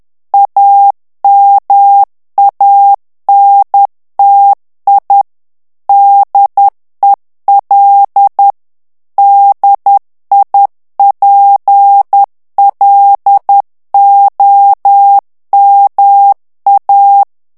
Morse code, heard as a keyed tone, about 10 wpm, 800 Hz. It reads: AMANTI DEL DIPLOMA
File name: morse mp3 / ITALIEN texte clair 1 vitesse 10 mots minute